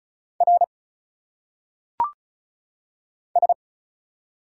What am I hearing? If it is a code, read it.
Morse code, heard as a keyed tone, about 35 wpm, 700 Hz. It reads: R S
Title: Single Letters - Phonetic Pronunciation 35wpm